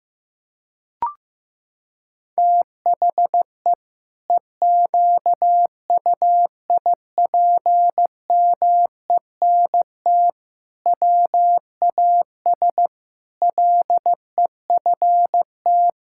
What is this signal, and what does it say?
Morse code, heard as a keyed tone, about 15 wpm, 700 Hz. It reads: THE EQUIPMENT WAS LEFT
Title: Sentences from Top 1000 Words - Review Part 5 - 15wpm